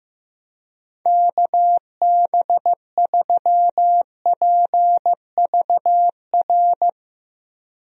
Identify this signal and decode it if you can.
Morse code, heard as a keyed tone, about 16 wpm, 700 Hz. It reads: KB3PVR
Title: USA Call Signs - 6 Characters - Repeated 2x - 15wpm